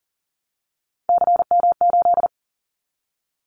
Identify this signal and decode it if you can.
Morse code, heard as a keyed tone, about 40 wpm, 700 Hz. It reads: CM8
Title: Callsign Prefixes for Top 100 DXCC - 40wpm